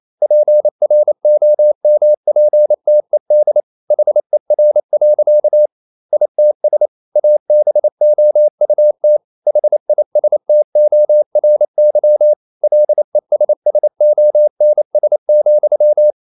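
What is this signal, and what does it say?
Morse code, heard as a keyed tone, about 28 wpm, 600 Hz. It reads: PROMPTED HER. ITS ABOUT HISTORY LESSONS,
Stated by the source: Morse code practice at 28 wpm, text of Ken Follett's novel Never in English, from CW med Gnister.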